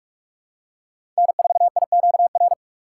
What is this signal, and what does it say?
Morse code, heard as a keyed tone, about 43 wpm, 700 Hz. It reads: N4IQR
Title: USA Call Signs - 4x Character Spacing - 45wpm